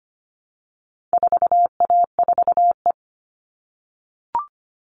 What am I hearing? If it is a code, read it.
Morse code, heard as a keyed tone, about 25 wpm, 700 Hz. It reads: 4A4E
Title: All Cut Numbers - 4-Digit 25wpm